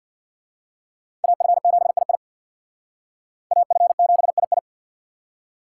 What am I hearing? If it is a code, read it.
Morse code, heard as a keyed tone, about 55 wpm, 700 Hz. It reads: AF7II AF7II
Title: Speed Racing - USA 2x2 Call Signs 50wpm